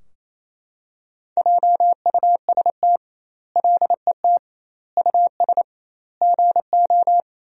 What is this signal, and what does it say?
Morse code, heard as a keyed tone, about 28 wpm, 700 Hz. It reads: JUST LET US GO
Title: 2x Word-Spacing - Sentences from Top 500 Words - 28wpm